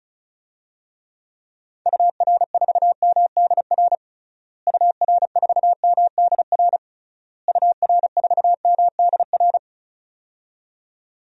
Morse code, heard as a keyed tone, about 34 wpm, 700 Hz. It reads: UR4MDR UR4MDR UR4MDR
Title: DXCC-100 Call Signs - 6 Characters - Repeated 3x - 35wpm